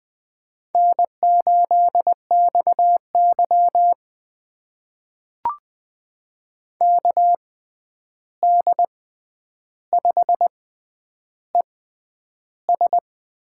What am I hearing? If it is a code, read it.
Morse code, heard as a keyed tone, about 21 wpm, spaced as for 10 wpm, 700 Hz. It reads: N8XY KD5ES
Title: USA Call Signs - 6x Character Spacing - 20wpm